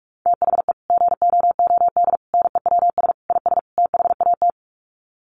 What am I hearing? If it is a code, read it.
Morse code, heard as a keyed tone, about 44 wpm, 700 Hz. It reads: THE GOOD NEWS IS THAT
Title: Sets of 5 Words 45wpm